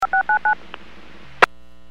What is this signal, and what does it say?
DTMF, heard as telephone touch tones, tones of about 110 ms. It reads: 5699
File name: WPS / 01373 17-53 WPS 451.25 WPS